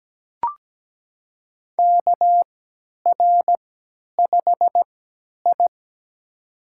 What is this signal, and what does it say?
Morse code, heard as a keyed tone, about 17 wpm, spaced as for 10 wpm, 700 Hz. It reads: KR5I
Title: USA Call Signs - 4 Characters - 3x Character Spacing - 17wpm